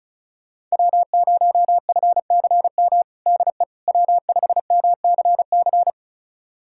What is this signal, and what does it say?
Morse code, heard as a keyed tone, about 36 wpm, 700 Hz. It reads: W0FCM DE W5MCC
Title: USA Callsign DE Callsign - 5 Characters - 35wpm